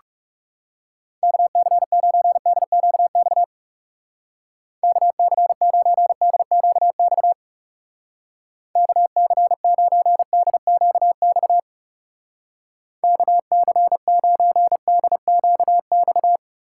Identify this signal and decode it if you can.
Morse code, heard as a keyed tone, about 38 wpm, 700 Hz. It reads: KC9DQX KC9DQX KC9DQX KC9DQX